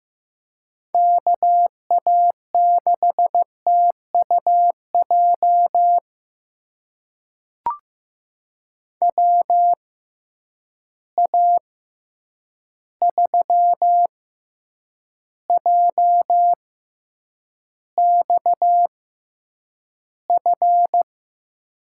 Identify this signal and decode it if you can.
Morse code, heard as a keyed tone, about 15 wpm, spaced as for 8 wpm, 700 Hz. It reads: KA6TUJ WA3JXF
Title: USA Call Signs - 6 Characters - 6x Character Spacing - 15wpm